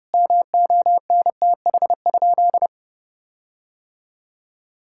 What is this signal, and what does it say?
Morse code, heard as a keyed tone, about 32 wpm, 700 Hz. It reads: MONTH?
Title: Sentences from Top 1600 Words 30wpm